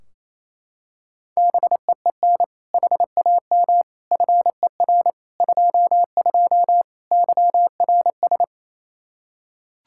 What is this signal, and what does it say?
Morse code, heard as a keyed tone, about 28 wpm, 700 Hz. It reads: BEEN HAM FER 22 YRS
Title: Practice QSO - Exchange 3 - More Info - 28wpm